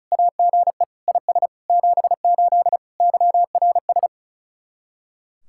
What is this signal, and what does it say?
Morse code, heard as a keyed tone, about 33 wpm, 700 Hz. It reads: AGE IS 78 YRS
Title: Speed-Racing Common QSO Phrases Encore - 35wpm